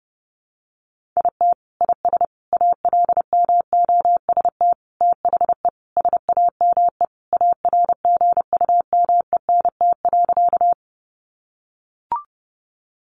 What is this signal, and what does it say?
Morse code, heard as a keyed tone, about 30 wpm, 700 Hz. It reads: IT IS ALMOST THE SAME ARGUMENT.
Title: Sentences from Top 1400 Words 30wpm